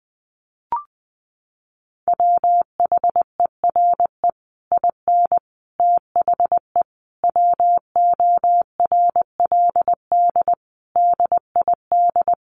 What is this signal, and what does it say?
Morse code, heard as a keyed tone, about 20 wpm, 700 Hz. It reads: WHERE IN THE WORLD DID
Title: Sentences from Top 200 Words 20wpm